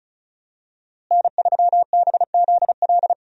Morse code, heard as a keyed tone, about 36 wpm, 700 Hz. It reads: N3BZL
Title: USA Call Signs - 5 Characters - Repeated 3x - 35wpm